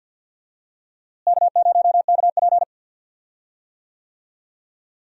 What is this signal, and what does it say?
Morse code, heard as a keyed tone, about 52 wpm, 700 Hz. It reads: K0KP
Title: USA Call Signs - 4 Characters - Repeated 3x - 50wpm